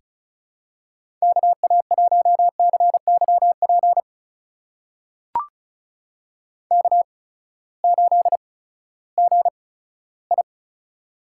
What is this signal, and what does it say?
Morse code, heard as a keyed tone, about 36 wpm, spaced as for 18 wpm, 700 Hz. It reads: KA1CYP K8GI
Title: USA Call Signs - 8x Character Spacing - 35wpm